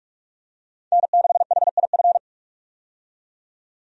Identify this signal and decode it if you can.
Morse code, heard as a keyed tone, about 46 wpm, 700 Hz. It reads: N6HIF
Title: USA Call Signs - 5 Characters - 45wpm